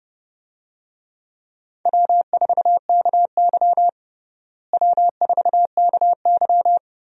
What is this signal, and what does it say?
Morse code, heard as a keyed tone, about 30 wpm, 700 Hz. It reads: W4KY W4KY
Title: USA Call Signs - 4 Characters - Repeated 2x - 30wpm